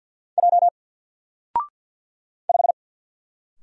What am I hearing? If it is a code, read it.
Morse code, heard as a keyed tone, about 48 wpm, 700 Hz. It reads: J 5
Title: Single Letter-Number Alt 50wpm